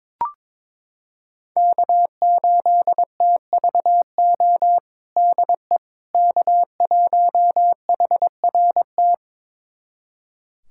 Morse code, heard as a keyed tone, about 22 wpm, 700 Hz. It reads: K8TVO DE K1HRT
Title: USA Callsign DE Callsign - 5 Characters - 22wpm